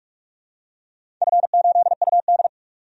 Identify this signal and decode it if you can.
Morse code, heard as a keyed tone, about 46 wpm, 700 Hz. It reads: F8UD
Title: DXCC-100 Call Signs - 4 Characters - 45wpm